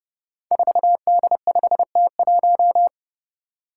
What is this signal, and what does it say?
Morse code, heard as a keyed tone, about 30 wpm, 700 Hz. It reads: 4D5T1
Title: Five Challenging Letter-Number Combinations 30wpm